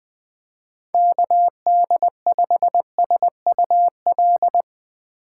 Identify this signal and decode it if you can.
Morse code, heard as a keyed tone, about 20 wpm, 700 Hz. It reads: KD5SUL
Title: Speed Racing - USA 2x3 Call Signs 17wpm